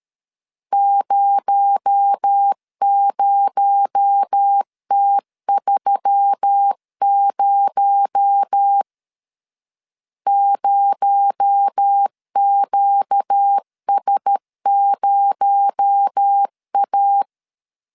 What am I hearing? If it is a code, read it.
Morse code, heard as a keyed tone, about 13 wpm, 800 Hz. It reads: 00T30 0QS0A
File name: LEZIONI / 6 PKX0 / 6 PKX0 12wpm 2231